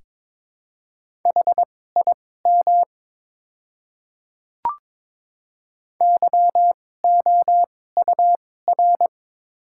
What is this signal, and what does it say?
Morse code, heard as a keyed tone, about 22 wpm, spaced as for 16 wpm, 690 Hz. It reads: HIM YOUR
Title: 2x Character Spacing - Top 100 Words - 22wpm